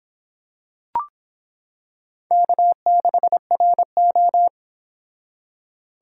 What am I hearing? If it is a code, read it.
Morse code, heard as a keyed tone, about 25 wpm, 700 Hz. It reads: K6RO